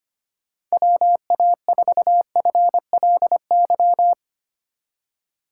Morse code, heard as a keyed tone, about 25 wpm, 700 Hz. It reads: WA4FLY
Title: USA Call Signs - 6 Characters - 4x Character Spacing - 25wpm